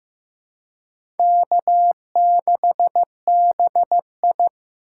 Morse code, heard as a keyed tone, about 15 wpm, 700 Hz. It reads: K6BI
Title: USA Call Signs - 4 Characters - 2x Character Spacing - 15wpm